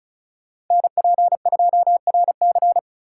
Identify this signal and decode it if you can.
Morse code, heard as a keyed tone, about 43 wpm, 700 Hz. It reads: NP2RC
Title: Speed Racing - USA 2x2 Call Signs 35wpm